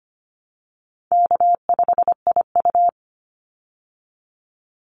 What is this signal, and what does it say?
Morse code, heard as a keyed tone, about 25 wpm, 700 Hz. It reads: K5IU
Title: Easy Call Signs Repeated 2x - 25wpm